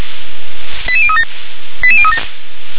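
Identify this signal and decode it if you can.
Five-tone selective call (ZVEI), heard as quick runs of five tones, tones of about 70 ms. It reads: 88037 88037
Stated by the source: ZVEI five-tone selective call 88013, received by SDR.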